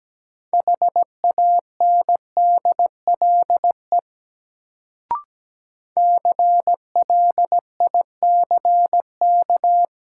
Morse code, heard as a keyed tone, about 17 wpm, 700 Hz. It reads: HANDLE CLICK
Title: Sentences from Top 1100 Words - New Word Drill - 17wpm